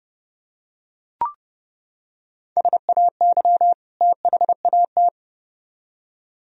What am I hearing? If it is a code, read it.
Morse code, heard as a keyed tone, about 29 wpm, 700 Hz. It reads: SAY THAT